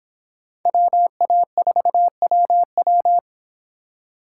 Morse code, heard as a keyed tone, about 30 wpm, 700 Hz. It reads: WA4WW